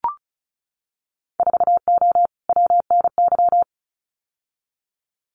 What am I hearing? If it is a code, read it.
Morse code, heard as a keyed tone, about 36 wpm, 700 Hz. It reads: 4O WNY
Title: Winter Field Day Exchanges Encore 35wpm